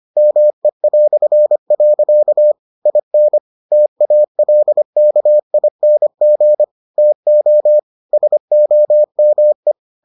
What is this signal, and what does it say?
Morse code, heard as a keyed tone, about 25 wpm, 600 Hz. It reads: ME". IN TALKING TO SOME